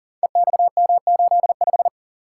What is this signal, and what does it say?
Morse code, heard as a keyed tone, about 39 wpm, 700 Hz. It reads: EXM85
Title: Code Groups - 5 Letter:Numbers - 40wpm